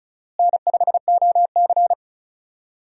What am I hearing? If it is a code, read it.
Morse code, heard as a keyed tone, about 36 wpm, 700 Hz. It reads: N5OC
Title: USA Call Signs - 4 Characters - Repeated 2x - 35wpm